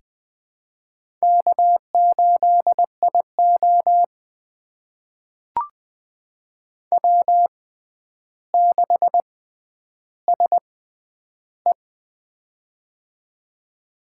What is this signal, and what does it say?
Morse code, heard as a keyed tone, about 20 wpm, spaced as for 10 wpm, 700 Hz. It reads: K8IO W6SE
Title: USA Call Signs - 4 Characters - 6x Character Spacing - 20wpm